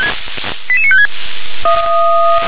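Five-tone selective call (ZVEI), heard as quick runs of five tones, tones of about 70 ms. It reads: 88956
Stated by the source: ZVEI five-tone call received by SDR.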